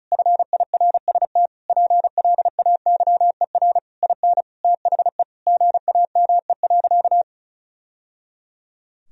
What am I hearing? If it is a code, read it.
Morse code, heard as a keyed tone, about 36 wpm, 700 Hz. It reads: FIRST PLAYER IN THE GAME.
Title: Sentences from Top 1000 Words - Review Part 4 - 35wpm